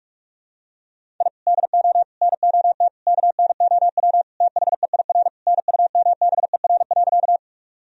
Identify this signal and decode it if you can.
Morse code, heard as a keyed tone, about 44 wpm, 700 Hz. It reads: I DO NOT KNOW THEIR NUMBER.
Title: Sentences from Top 1000 Words - Review Part 1 - 45wpm